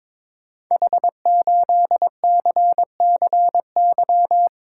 Morse code, heard as a keyed tone, about 23 wpm, 700 Hz. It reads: H8CCY
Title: Code Groups - 5 Letter:Numbers - 22wpm